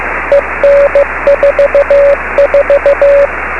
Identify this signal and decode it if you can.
Morse code, heard as a keyed tone, about 15 wpm, 560 Hz, with radio noise. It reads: EN44
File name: beacons / k9mu